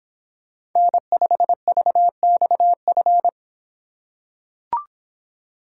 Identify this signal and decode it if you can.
Morse code, heard as a keyed tone, about 26 wpm, 700 Hz. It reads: N5VXF